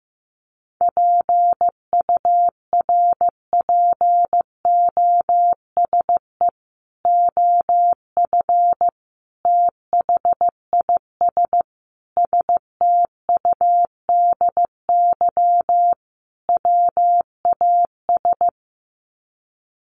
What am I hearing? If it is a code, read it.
Morse code, heard as a keyed tone, about 15 wpm, 700 Hz. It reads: PURPOSE OF THIS STUDY WAS
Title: Sets of 5 Words 15wpm